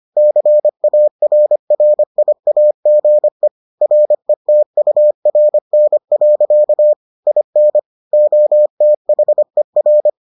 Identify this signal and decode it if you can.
Morse code, heard as a keyed tone, about 25 wpm, 600 Hz. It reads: CARRIAGE RETURN. IN OTHER